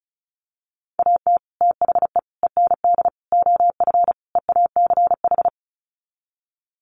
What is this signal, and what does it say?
Morse code, heard as a keyed tone, about 34 wpm, 700 Hz. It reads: AT THE END OF EACH
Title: Sets of 5 Words Encore 35wpm